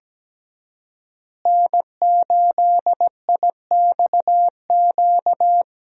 Morse code, heard as a keyed tone, about 17 wpm, 700 Hz. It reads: N8IXQ
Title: USA Call Signs - 5 Characters - Repeated 2x - 17wpm